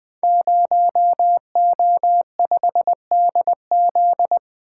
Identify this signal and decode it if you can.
Morse code, heard as a keyed tone, about 20 wpm, 700 Hz. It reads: 0O5DZ
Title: Code Groups - 5 Letter:Numbers - 20wpm